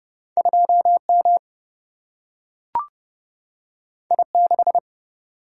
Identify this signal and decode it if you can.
Morse code, heard as a keyed tone, about 30 wpm, 700 Hz. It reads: 2 M I 6
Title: Single-Letter Single-Number 30wpm